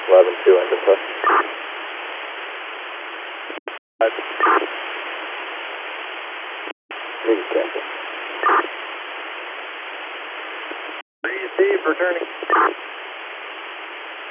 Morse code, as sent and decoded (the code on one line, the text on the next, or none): none